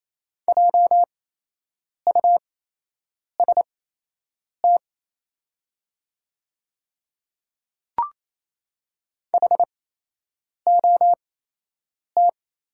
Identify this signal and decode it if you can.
Morse code, heard as a keyed tone, about 28 wpm, spaced as for 8 wpm, 700 Hz. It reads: JUST HOT
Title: 8x Character Spacing - Top 200 Words - 28wpm